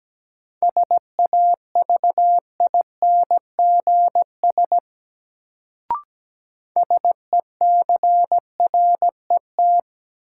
Morse code, heard as a keyed tone, about 17 wpm, 700 Hz. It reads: SAVINGS SECRET
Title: Sentences from Top 1200 Words - New Word Drill Alt - 17wpm